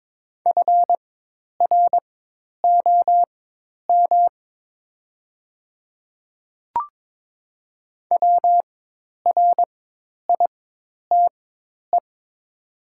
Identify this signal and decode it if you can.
Morse code, heard as a keyed tone, about 22 wpm, spaced as for 10 wpm, 700 Hz. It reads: FROM WRITE
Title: 4x Character Spacing - Top 200 Words - 22wpm